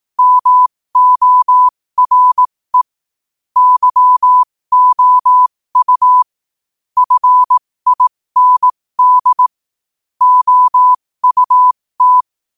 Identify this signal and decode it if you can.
Morse code, heard as a keyed tone, about 18 wpm, spaced as for 15 wpm, 1000 Hz. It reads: MORE YOU FIND OUT
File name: qotdfiles / 15wpm